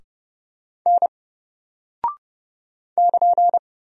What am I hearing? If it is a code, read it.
Morse code, heard as a keyed tone, about 30 wpm, 700 Hz. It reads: N <KN>
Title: Single Character (Letter:Number:Punctuation:Prosign) 30wpm